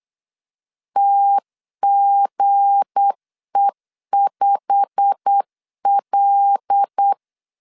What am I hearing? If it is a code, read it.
Morse code, heard as a keyed tone, about 8 wpm, 800 Hz. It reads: TGE5L
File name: LEZIONI / 10 SIGNS / 10B INT 8wpm 3534